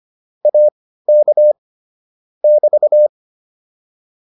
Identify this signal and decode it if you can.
Morse code, heard as a keyed tone, about 25 wpm, spaced as for 15 wpm, 600 Hz. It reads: AK <BT>
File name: QSO107 15